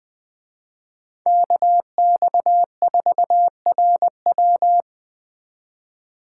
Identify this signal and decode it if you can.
Morse code, heard as a keyed tone, about 20 wpm, 700 Hz. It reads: KX4RW
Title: Speed Racing - USA 2x2 Call Signs 17wpm